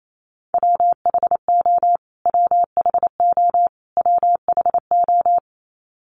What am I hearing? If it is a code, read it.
Morse code, heard as a keyed tone, about 28 wpm, 700 Hz. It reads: WHO WHO WHO
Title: Top 200 Words Repeated 28wpm